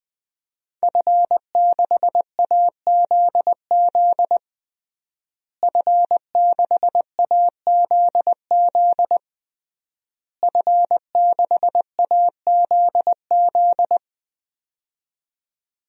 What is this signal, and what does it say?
Morse code, heard as a keyed tone, about 20 wpm, 700 Hz. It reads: F6AZZ F6AZZ F6AZZ
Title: DXCC-100 Call Signs - 5 Characters - Repeated 3x - 20wpm